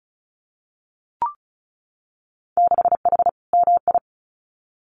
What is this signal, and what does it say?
Morse code, heard as a keyed tone, about 34 wpm, 700 Hz. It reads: 6H MI
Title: Winter Field Day Exchanges Encore 35wpm